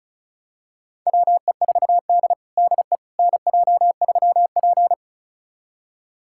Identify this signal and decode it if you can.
Morse code, heard as a keyed tone, about 36 wpm, 700 Hz. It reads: WE4D DE NJ3P
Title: USA Callsign DE Callsign - 4 Characters - 35wpm